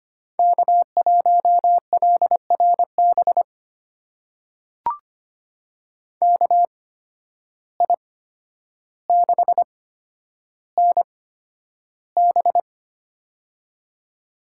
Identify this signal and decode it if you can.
Morse code, heard as a keyed tone, about 25 wpm, spaced as for 11 wpm, 700 Hz. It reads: K1LRB KI6NB
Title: USA Call Signs - 5 Characters - 8x Character Spacing - 25wpm